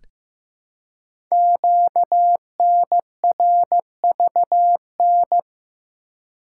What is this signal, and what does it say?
Morse code, heard as a keyed tone, about 15 wpm, 700 Hz. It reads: QNRVN
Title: Code Groups - 5 Letters - 15wpm